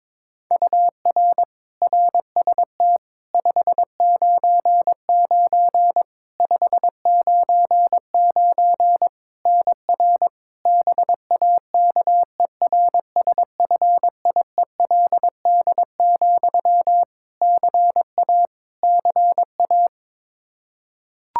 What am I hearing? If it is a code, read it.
Morse code, heard as a keyed tone, about 22 wpm, 700 Hz. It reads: UR RST 599 599 NR BAKERSFIELD, CA CA
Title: Practice QSO - Exchange 1 - Beginning - 22wpm